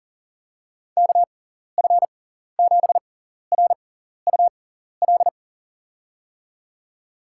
Morse code, heard as a keyed tone, about 40 wpm, spaced as for 14 wpm, 700 Hz. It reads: KF7RUL
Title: USA Call Signs - 6 Characters - 6x Character Spacing - 40wpm